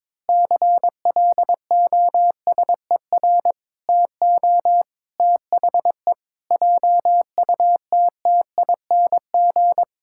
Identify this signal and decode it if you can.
Morse code, heard as a keyed tone, about 22 wpm, 700 Hz. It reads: CLOSER TO THE JUTTING
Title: Pangram Challenge v2 - 22wpm